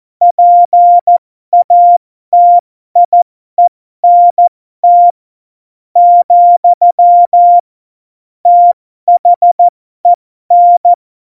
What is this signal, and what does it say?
Morse code, heard as a keyed tone, about 14 wpm, spaced as for 12 wpm, 700 Hz. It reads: PATIENT , THEN
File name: Morsekurs / Witze